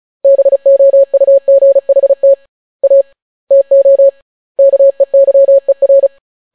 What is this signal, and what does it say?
Morse code, heard as a keyed tone, about 36 wpm, 550 Hz. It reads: BOUGHT A TO KEYER